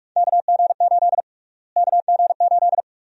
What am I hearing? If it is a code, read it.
Morse code, heard as a keyed tone, about 43 wpm, 700 Hz. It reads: KG8 KG8
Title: USA Call Sign Prefixes - Repeated 2x - 45wpm